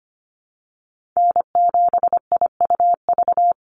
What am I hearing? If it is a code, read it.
Morse code, heard as a keyed tone, about 25 wpm, 700 Hz. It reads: N7IUV
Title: USA Call Signs 25wpm